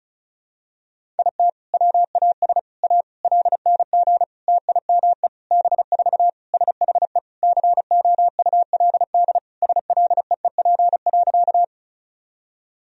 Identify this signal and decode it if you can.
Morse code, heard as a keyed tone, about 36 wpm, 700 Hz. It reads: IT WAS A LNG TIME B4 SHE COULD SLEEP.
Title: Sentences from Top 500 Words and Abbreviations 35wpm